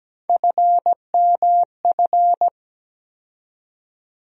Morse code, heard as a keyed tone, about 17 wpm, 700 Hz. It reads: FMF
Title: USA Call Sign Postfixes - 3 Letters 17wpm